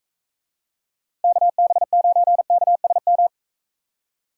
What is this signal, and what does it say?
Morse code, heard as a keyed tone, about 40 wpm, 700 Hz. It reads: KB9KSM